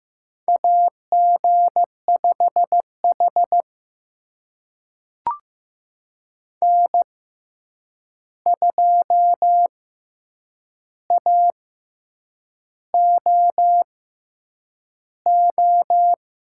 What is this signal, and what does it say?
Morse code, heard as a keyed tone, about 15 wpm, spaced as for 7 wpm, 700 Hz. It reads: AG5H N2AOO
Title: USA Call Signs - 6x Character Spacing - 15wpm